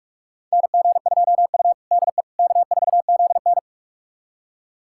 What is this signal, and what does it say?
Morse code, heard as a keyed tone, about 46 wpm, 700 Hz. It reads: NG2U DE K4ZN
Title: USA Callsign DE Callsign - 4 Characters - 45wpm